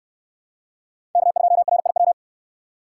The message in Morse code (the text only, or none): K2RIW